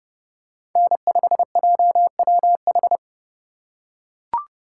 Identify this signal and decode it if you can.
Morse code, heard as a keyed tone, about 31 wpm, 700 Hz. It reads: N5JWH
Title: USA Call Signs - 5 Characters - 2x Character Spacing - 30wpm